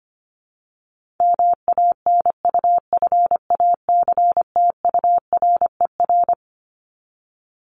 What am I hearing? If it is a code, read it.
Morse code, heard as a keyed tone, about 25 wpm, 700 Hz. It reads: MANUFACTURER
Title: Top 1751-2000 Words 25wpm